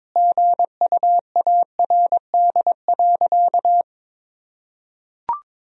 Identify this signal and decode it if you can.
Morse code, heard as a keyed tone, about 22 wpm, 700 Hz. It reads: GUARD.
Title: Sentences from Top 1000 Words - Review Part 4 - 22wpm